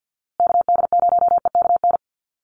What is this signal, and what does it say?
Morse code, heard as a keyed tone, about 48 wpm, 700 Hz. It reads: KD0EKN
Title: USA Call Signs Repeated 3x 50wpm